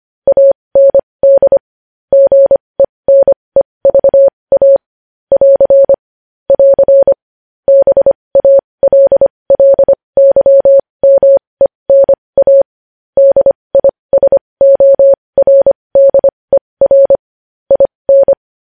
Morse code, heard as a keyed tone, about 25 wpm, spaced as for 20 wpm, 550 Hz. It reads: AND GENEVA <AR> <AR> BALLYMENA DISORDER IN
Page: News Headlines in Morse Code at 20 WPM